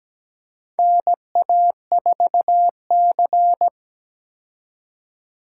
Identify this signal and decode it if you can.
Morse code, heard as a keyed tone, about 17 wpm, 700 Hz. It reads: NA4C
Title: K1USN SST QSO Practice - 17wpm